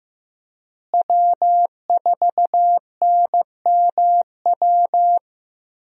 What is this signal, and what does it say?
Morse code, heard as a keyed tone, about 15 wpm, 700 Hz. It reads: W4NMW
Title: USA Call Signs - 5 Characters - Repeated 3x - 15wpm